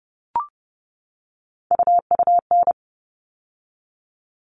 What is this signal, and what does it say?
Morse code, heard as a keyed tone, about 30 wpm, 700 Hz. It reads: UUN
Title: All Cut Numbers - 3-Digit 30wpm